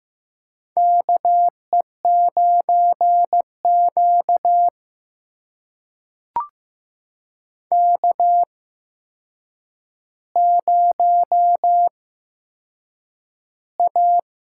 Morse code, heard as a keyed tone, about 15 wpm, spaced as for 7 wpm, 700 Hz. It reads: KE9Q K0A
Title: USA Call Signs - 8x Character Spacing - 15wpm